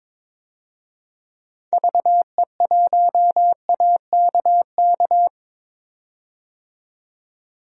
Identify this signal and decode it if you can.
Morse code, heard as a keyed tone, about 22 wpm, 700 Hz. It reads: VE1AKK
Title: DXCC-100 Call Signs - 6 Characters - 22wpm